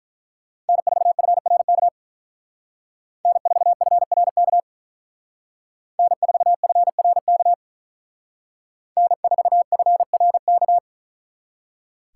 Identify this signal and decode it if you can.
Morse code, heard as a keyed tone, about 43 wpm, 700 Hz. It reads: N4FRK N4FRK N4FRK N4FRK
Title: Speed Racing - USA 1x3 Call Signs 35wpm